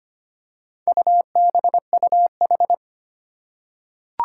Morse code, heard as a keyed tone, about 25 wpm, 700 Hz. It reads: UBUH
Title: Four Challenging Letter-Number Combinations 25wpm